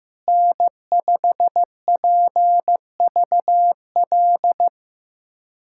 Morse code, heard as a keyed tone, about 16 wpm, 700 Hz. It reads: N5PVL
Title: USA Callsign DE Callsign 15wpm